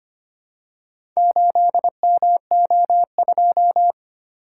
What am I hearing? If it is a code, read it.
Morse code, heard as a keyed tone, about 25 wpm, 690 Hz. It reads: 8MO2
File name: practice / 25wpm / Four Letter-Number Combinations - 25wpm